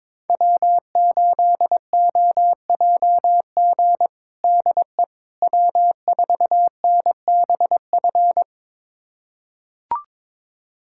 Morse code, heard as a keyed tone, about 21 wpm, 700 Hz. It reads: W8OJG DE W4NBF
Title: USA Callsign DE Callsign - 5 Characters - 22wpm